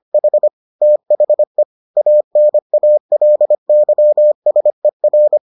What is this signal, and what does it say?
Morse code, heard as a keyed tone, about 25 wpm, 600 Hz. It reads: H THE ANALYSER